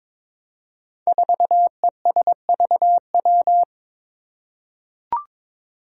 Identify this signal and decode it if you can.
Morse code, heard as a keyed tone, about 21 wpm, 700 Hz. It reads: 4ESVW
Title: Five Challenging Letter-Number Combinations 22wpm